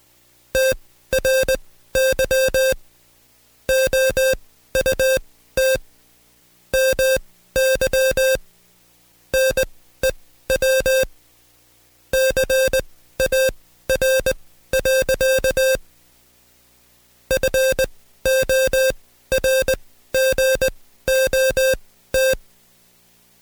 Morse code, heard as a keyed tone, about 20 wpm, spaced as for 13 wpm, 530 Hz. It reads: TRY OUT MY NEW CAR. FORGOT
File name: Code Quick / AudioDownld2 / Seg133-Stepper6 13WPM